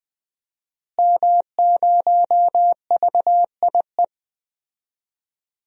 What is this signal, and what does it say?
Morse code, heard as a keyed tone, about 21 wpm, 700 Hz. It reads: M0VIE
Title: DXCC-100 Call Signs - 5 Characters - Repeated 2x - 20wpm